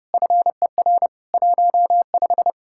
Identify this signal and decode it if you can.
Morse code, heard as a keyed tone, about 40 wpm, 700 Hz. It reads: FER 15